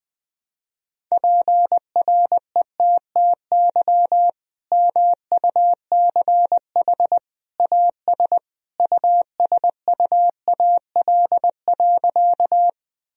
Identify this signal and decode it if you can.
Morse code, heard as a keyed tone, about 20 wpm, 700 Hz. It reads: PRETTY MUCH AS USUAL.
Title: Sentences from Top 1000 Words - Review Part 2 - 20wpm